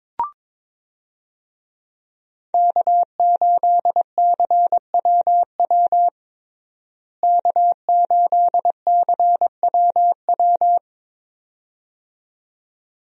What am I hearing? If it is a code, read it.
Morse code, heard as a keyed tone, about 22 wpm, 700 Hz. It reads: K8CWW K8CWW
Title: USA Call Signs - 5 Characters - Repeated 2x - 22wpm